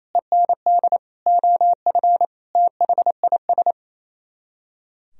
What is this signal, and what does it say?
Morse code, heard as a keyed tone, about 34 wpm, 700 Hz. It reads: END OF THIS
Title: Speed-Racing - Sets of 5 Words 28wpm